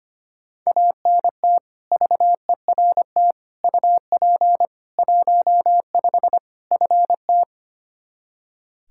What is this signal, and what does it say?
Morse code, heard as a keyed tone, about 25 wpm, 700 Hz. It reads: ANT VERT UP 15 FT
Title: Practice QSO - Exchange 2 - Info - 25wpm